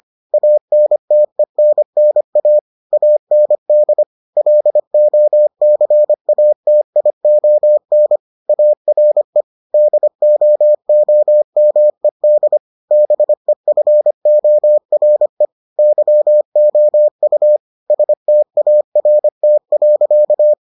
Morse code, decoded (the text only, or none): ANTENNA AND LOCATION ARE DOOMED BEFORE YOU START.